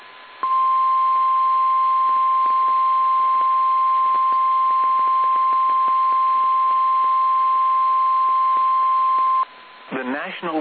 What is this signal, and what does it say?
Emergency Alert System (SAME) tones (the 1050 Hz attention tone), with speech. It is heard from 0.4 to 9.5 s.
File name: eas / KSDZKDJL / KSDZKDJL